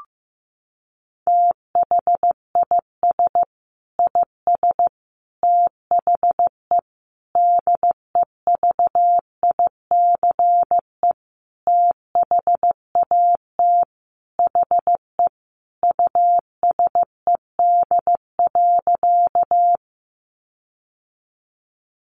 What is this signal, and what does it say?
Morse code, heard as a keyed tone, about 15 wpm, 700 Hz. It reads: THIS IS THE DEVICE THAT HE USED.
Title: Sentences from Top 1500 Words 15wpm